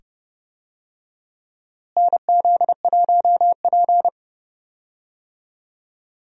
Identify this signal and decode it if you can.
Morse code, heard as a keyed tone, about 31 wpm, 700 Hz. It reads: NZ1P
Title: USA Call Signs - 4 Characters - Repeated 2x - 30wpm